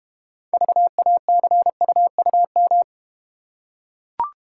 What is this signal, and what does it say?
Morse code, heard as a keyed tone, about 32 wpm, 700 Hz. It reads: VACUUM